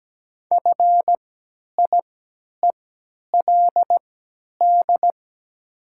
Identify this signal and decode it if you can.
Morse code, heard as a keyed tone, about 17 wpm, spaced as for 10 wpm, 700 Hz. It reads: FIELD
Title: 3x Character Spacing - Top 500 Words - 17wpm